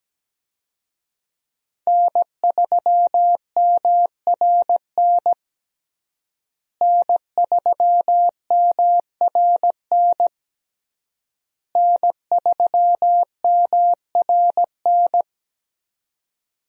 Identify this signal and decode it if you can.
Morse code, heard as a keyed tone, about 17 wpm, 700 Hz. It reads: N3MRN N3MRN N3MRN
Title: USA Call Signs - 5 Characters - Repeated 3x - 17wpm